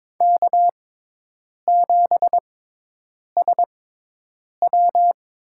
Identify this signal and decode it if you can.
Morse code, heard as a keyed tone, about 22 wpm, spaced as for 8 wpm, 700 Hz. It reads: K7SW
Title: USA Call Signs - 4 Characters - 6x Character Spacing - 22wpm